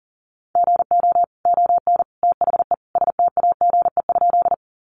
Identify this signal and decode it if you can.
Morse code, heard as a keyed tone, about 40 wpm, 700 Hz. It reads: GO ON THE STAGE?
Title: Sentences from Top 1200 Words 40wpm